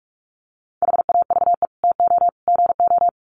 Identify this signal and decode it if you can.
Morse code, heard as a keyed tone, about 44 wpm, 700 Hz. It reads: HAVE TO GO
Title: Sets of 3 Words 45wpm